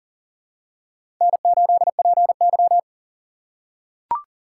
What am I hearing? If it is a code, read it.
Morse code, heard as a keyed tone, about 40 wpm, 700 Hz. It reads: N8PY
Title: USA Call Signs - 4 Characters - 6x Character Spacing - 40wpm